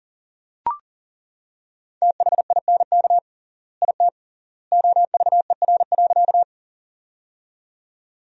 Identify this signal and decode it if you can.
Morse code, heard as a keyed tone, about 40 wpm, 700 Hz. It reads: THINK IT OVER.
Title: 3x Word-Spacing - Sentences from Top 200 Words - 40wpm